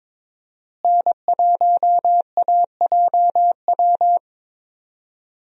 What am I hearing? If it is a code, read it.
Morse code, heard as a keyed tone, about 22 wpm, 700 Hz. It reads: N1AJW
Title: USA Call Signs - 5 Characters - Repeated 2x - 22wpm